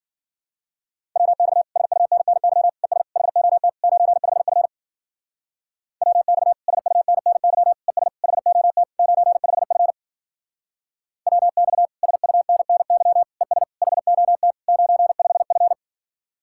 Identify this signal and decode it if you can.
Morse code, heard as a keyed tone, about 50 wpm, 700 Hz. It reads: WX SUNNY ES HOT 95F WX SUNNY ES HOT 95F WX SUNNY ES HOT 95F